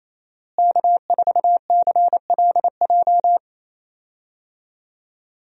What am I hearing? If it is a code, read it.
Morse code, heard as a keyed tone, about 28 wpm, 700 Hz. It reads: K4CLJ
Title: USA Call Signs - 5 Characters - Repeated 3x - 28wpm